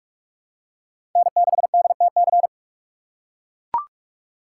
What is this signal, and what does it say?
Morse code, heard as a keyed tone, about 42 wpm, 700 Hz. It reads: N6DTC
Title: USA Call Signs - 5 Characters - 4x Character Spacing - 45wpm